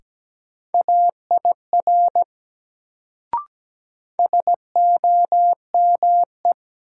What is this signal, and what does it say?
Morse code, heard as a keyed tone, about 17 wpm, 700 Hz. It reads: AIR SOME
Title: Sets of 2 Words - New Word Drill - 17wpm